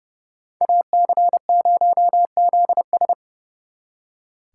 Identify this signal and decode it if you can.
Morse code, heard as a keyed tone, about 36 wpm, 700 Hz. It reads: AC0ZS
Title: Speed Racing - USA 2x2 Call Signs 30wpm